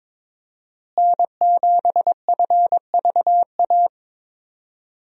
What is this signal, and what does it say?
Morse code, heard as a keyed tone, about 21 wpm, 700 Hz. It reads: N7FVA
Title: USA Call Signs - 5 Characters - Repeated 2x - 22wpm